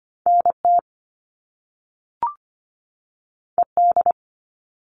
Morse code, heard as a keyed tone, about 25 wpm, 700 Hz. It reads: N T E D
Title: All Cut Numbers - 2-Digit 25wpm